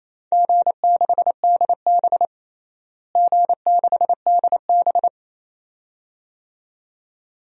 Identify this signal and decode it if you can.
Morse code, heard as a keyed tone, about 28 wpm, 700 Hz. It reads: G6DB G6DB
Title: DXCC-100 Call Signs - 4 Characters - Repeated 2x - 28wpm